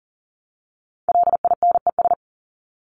Morse code, heard as a keyed tone, about 39 wpm, 700 Hz. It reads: LINES
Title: ES:S Suffix - Top 100 Words with Base Words 40wpm